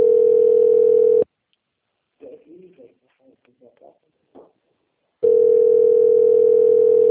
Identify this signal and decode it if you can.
Ringback tone (North America), 440 + 480 Hz, with speech.
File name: out- - - - - .740